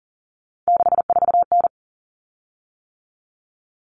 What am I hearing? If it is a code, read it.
Morse code, heard as a keyed tone, about 40 wpm, 700 Hz. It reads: 64N
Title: Common Cut Numbers - 3-Digit 40wpm